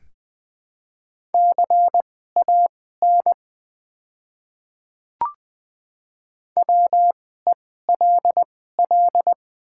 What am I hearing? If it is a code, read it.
Morse code, heard as a keyed tone, about 20 wpm, spaced as for 15 wpm, 700 Hz. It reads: CAN WELL
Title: 2x Character Spacing - Top 200 Words - 20wpm